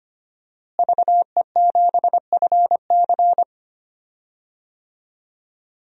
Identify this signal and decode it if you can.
Morse code, heard as a keyed tone, about 25 wpm, 700 Hz. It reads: VE7FC
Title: DXCC-100 Call Signs - 5 Characters - Repeated 2x - 25wpm